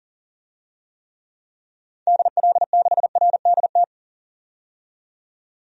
Morse code, heard as a keyed tone, about 40 wpm, 700 Hz. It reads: DL6RDT
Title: DXCC-100 Call Signs - 6 Characters - 40wpm